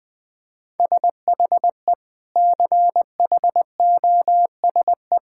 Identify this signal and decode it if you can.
Morse code, heard as a keyed tone, about 20 wpm, 700 Hz. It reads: SHE CHOSE
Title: Sentences with 3-Word Phrases - Spoken Beginning - 20wpm